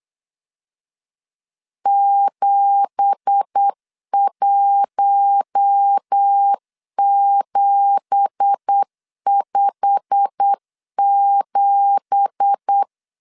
Morse code, heard as a keyed tone, about 8 wpm, 800 Hz. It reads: 71757
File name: LEZIONI / 9 2867 8wpm 3440